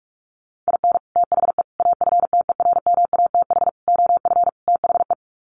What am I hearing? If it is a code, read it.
Morse code, heard as a keyed tone, about 44 wpm, 700 Hz. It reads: IN THE AFTERMATH OF THE
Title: Sets of 5 Words 45wpm